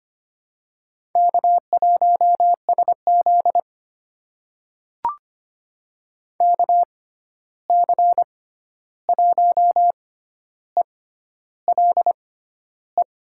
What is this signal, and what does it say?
Morse code, heard as a keyed tone, about 25 wpm, spaced as for 11 wpm, 700 Hz. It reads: K1SZ KC1ELE
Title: USA Call Signs - 6x Character Spacing - 25wpm